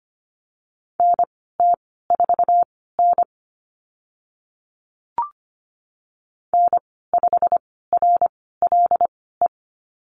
Morse code, heard as a keyed tone, about 25 wpm, spaced as for 16 wpm, 690 Hz. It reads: NT4N N5RLE
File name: Lesson 15g - Number 5 - Call Signs 25wpm